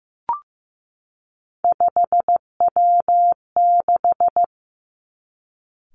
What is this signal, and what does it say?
Morse code, heard as a keyed tone, about 15 wpm, 700 Hz. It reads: 5W6
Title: Three Challenging Letter-Number Combinations 15wpm